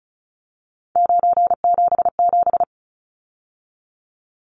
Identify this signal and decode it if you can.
Morse code, heard as a keyed tone, about 34 wpm, 700 Hz. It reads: 977
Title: Numbers - 3-Digit 35wpm